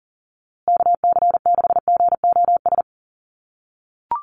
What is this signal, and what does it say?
Morse code, heard as a keyed tone, about 40 wpm, 700 Hz. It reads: KC6GOS
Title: USA Call Signs Repeated 3x 40wpm